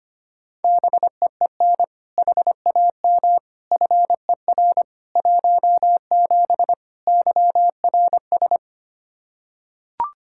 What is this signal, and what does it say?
Morse code, heard as a keyed tone, about 25 wpm, 700 Hz. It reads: BEEN HAM FER 17 YRS
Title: Practice QSO - Exchange 3 - More Info - 25wpm